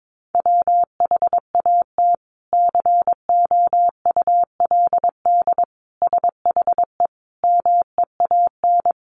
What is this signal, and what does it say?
Morse code, heard as a keyed tone, about 22 wpm, 700 Hz. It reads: WHAT COULD SHE MEAN
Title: Sentences from Top 200 Words 22wpm